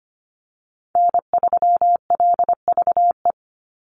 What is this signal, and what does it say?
Morse code, heard as a keyed tone, about 25 wpm, 700 Hz. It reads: N3LVE
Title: USA Call Signs 25wpm